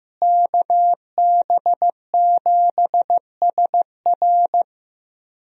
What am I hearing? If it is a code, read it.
Morse code, heard as a keyed tone, about 15 wpm, 700 Hz. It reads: KB7SR
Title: USA Call Signs - 5 Characters - Repeated 3x - 15wpm